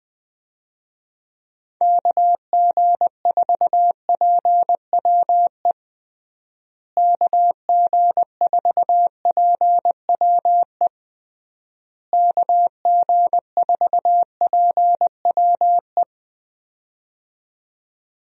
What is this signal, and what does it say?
Morse code, heard as a keyed tone, about 20 wpm, 700 Hz. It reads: KG4PWE KG4PWE KG4PWE
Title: USA Call Signs - 6 Characters - Repeated 3x - 20wpm